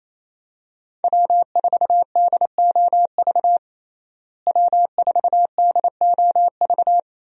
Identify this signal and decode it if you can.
Morse code, heard as a keyed tone, about 28 wpm, 700 Hz. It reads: W4DOV W4DOV
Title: USA Call Signs - 5 Characters - Repeated 3x - 28wpm